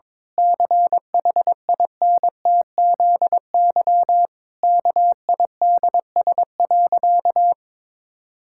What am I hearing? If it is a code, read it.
Morse code, heard as a keyed tone, about 22 wpm, 700 Hz. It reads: CHINTZY KIDS.
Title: Pangram Challenge v2 - 22wpm